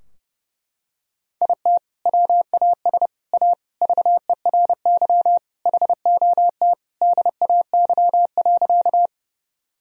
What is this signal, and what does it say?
Morse code, heard as a keyed tone, about 32 wpm, 700 Hz. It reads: IT WAS A VERY HOT DAY.
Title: Sentences from Top 1000 Words - Review Part 3 - 30wpm